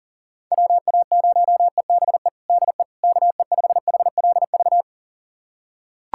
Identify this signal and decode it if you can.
Morse code, heard as a keyed tone, about 40 wpm, 700 Hz. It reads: WA0EBE DE KE5HLV
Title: USA Callsign DE Callsign - 6 Characters - 40wpm